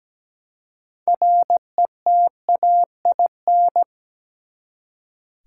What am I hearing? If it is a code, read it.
Morse code, heard as a keyed tone, about 17 wpm, 700 Hz. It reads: RETAIN
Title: Six Letter Words - 17wpm